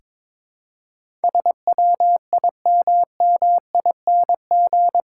Morse code, heard as a keyed tone, about 21 wpm, 700 Hz. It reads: SWIMMING
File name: practice / 22wpm / Eight Letter Words - 22wpm